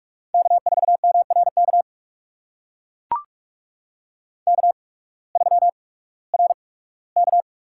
Morse code, heard as a keyed tone, about 42 wpm, spaced as for 21 wpm, 700 Hz. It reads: K4MRK K3RK
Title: USA Call Signs - 8x Character Spacing - 45wpm